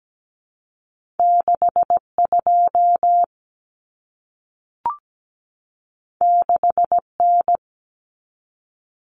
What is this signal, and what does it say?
Morse code, heard as a keyed tone, about 17 wpm, 700 Hz. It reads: 62 6N
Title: Common Cut Numbers - 2-Digit 17wpm